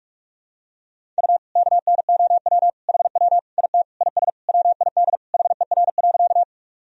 Morse code, heard as a keyed tone, about 46 wpm, 700 Hz. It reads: U KNOW HW IT IS WID HER.
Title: Sentences from Top 100 Words and Abbreviations 45wpm